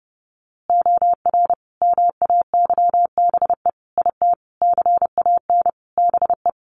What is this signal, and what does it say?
Morse code, heard as a keyed tone, about 30 wpm, 700 Hz. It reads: OR MAYBE IT CAN BE
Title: A Sound Of Thunder 30wpm